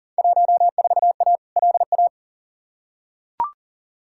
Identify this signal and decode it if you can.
Morse code, heard as a keyed tone, about 40 wpm, 700 Hz. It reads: 14A LA
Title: Field Day Exchanges Encore - 40wpm